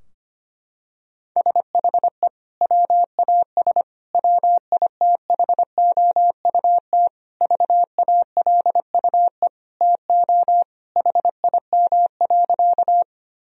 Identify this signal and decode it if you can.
Morse code, heard as a keyed tone, about 25 wpm, 700 Hz. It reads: SHE WAS WITHOUT VALUE TO HIM.
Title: Sentences from Top 1000 Words - Review Part 3 - 25wpm